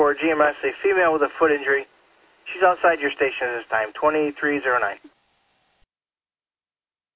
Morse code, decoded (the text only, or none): none